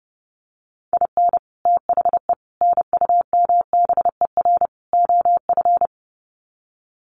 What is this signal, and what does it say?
Morse code, heard as a keyed tone, about 30 wpm, 700 Hz. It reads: IN THE NUMBER OF
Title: Sets of 4 Words 30wpm